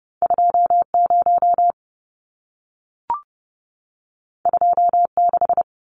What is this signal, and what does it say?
Morse code, heard as a keyed tone, about 30 wpm, 700 Hz. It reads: 20 26
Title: Numbers - 2-Digit 30wpm